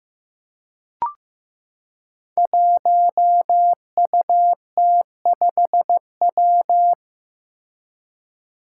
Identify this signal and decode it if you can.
Morse code, heard as a keyed tone, about 15 wpm, 700 Hz. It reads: 1UT5W
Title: Five Challenging Letter-Number Combinations 15wpm